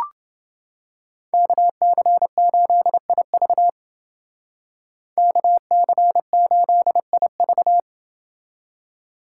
Morse code, heard as a keyed tone, about 29 wpm, 700 Hz. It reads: KC8IV KC8IV